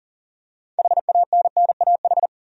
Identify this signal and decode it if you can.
Morse code, heard as a keyed tone, about 40 wpm, 700 Hz. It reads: HANNAH
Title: USA Common Names - 40wpm